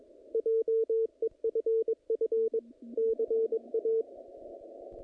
Morse code, heard as a keyed tone, about 23 wpm, 450 Hz, with radio noise. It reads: JEFF CA
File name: contests / naqp-cw-k5tr-op-k5pi- -0000